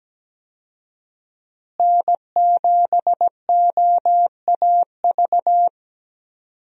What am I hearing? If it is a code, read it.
Morse code, heard as a keyed tone, about 17 wpm, 700 Hz. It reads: N7OAV
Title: USA Call Signs - 5 Characters - Repeated 2x - 17wpm